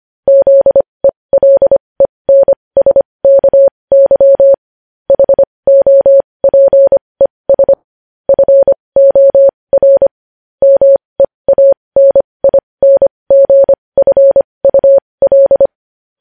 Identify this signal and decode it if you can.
Morse code, heard as a keyed tone, about 25 wpm, spaced as for 20 wpm, 550 Hz. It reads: ZELENSKY HOPES FOR MEANINGFUL
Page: News Headlines in Morse Code at 20 WPM